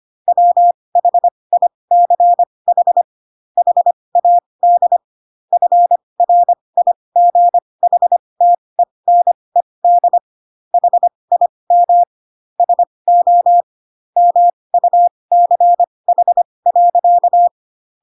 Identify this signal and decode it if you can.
Morse code, heard as a keyed tone, about 25 wpm, spaced as for 20 wpm, 700 Hz. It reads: WHICH HAD FRIGHTENED HIM SO MUCH.
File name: Pinocchio / Pinocchio ch2 15 25 ik1hge